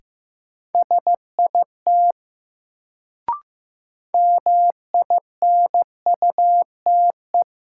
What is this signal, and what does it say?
Morse code, heard as a keyed tone, about 15 wpm, 700 Hz. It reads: SIT MINUTE
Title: Sentences from Top 500 Words - New Word Drill - 15wpm